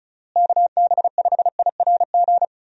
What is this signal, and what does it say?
Morse code, heard as a keyed tone, about 36 wpm, 700 Hz. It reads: KB5IRG
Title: USA Call Signs - 6 Characters - Repeated 3x - 35wpm